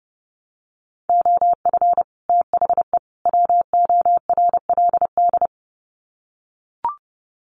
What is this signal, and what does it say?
Morse code, heard as a keyed tone, about 28 wpm, 700 Hz. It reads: OF THE WORLD
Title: Sets of 3 Words 30wpm